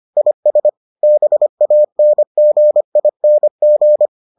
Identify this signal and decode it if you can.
Morse code, heard as a keyed tone, about 25 wpm, 600 Hz. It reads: IS BANGING